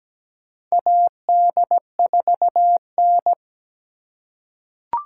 Morse code, heard as a keyed tone, about 17 wpm, 700 Hz. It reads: AD4N
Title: USA Call Signs - 4 Characters - 3x Character Spacing - 17wpm